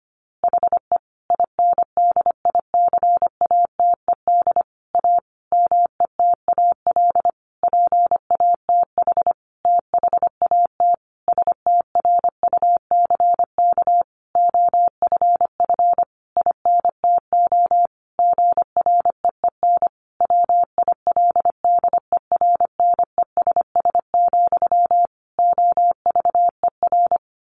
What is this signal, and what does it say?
Morse code, heard as a keyed tone, about 25 wpm, 700 Hz. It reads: HE INDICATED A METAL PATH THAT STRUCK OFF INTO GREEN WILDERNESS, OVER
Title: A Sound Of Thunder 25wpm